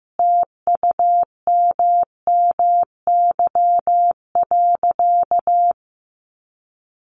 Morse code, heard as a keyed tone, about 15 wpm, 700 Hz. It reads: TUMMY.
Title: Easy Phrases 15wpm